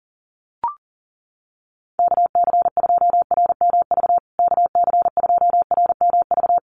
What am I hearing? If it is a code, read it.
Morse code, heard as a keyed tone, about 40 wpm, 700 Hz. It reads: KC2RMV KC2RMV
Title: USA Call Signs Repeated 3x 40wpm